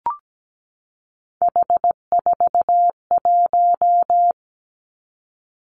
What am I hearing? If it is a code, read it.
Morse code, heard as a keyed tone, about 17 wpm, 700 Hz. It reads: H41
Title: Three Challenging Letter-Number Combinations 17wpm